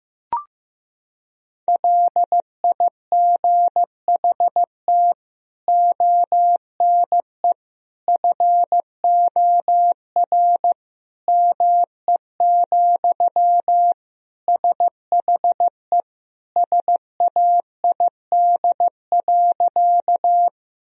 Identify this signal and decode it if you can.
Morse code, heard as a keyed tone, about 15 wpm, 700 Hz. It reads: LIGHT ONE FOR ME, SHE SAID.
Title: Sentences from Top 1000 Words - Review Part 1 - 15wpm